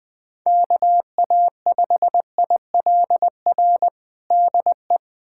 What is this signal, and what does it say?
Morse code, heard as a keyed tone, about 21 wpm, 700 Hz. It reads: KA5ILR DE
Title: USA Callsign DE Callsign - 6 Characters - 20wpm